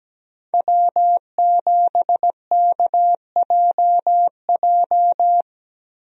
Morse code, heard as a keyed tone, about 17 wpm, 700 Hz. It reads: W7KJJ
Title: USA Call Signs - 5 Characters - Repeated 2x - 17wpm